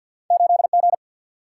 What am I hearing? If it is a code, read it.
Morse code, heard as a keyed tone, about 54 wpm, 700 Hz. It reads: 8G
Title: Single-Letter Single-Number 50wpm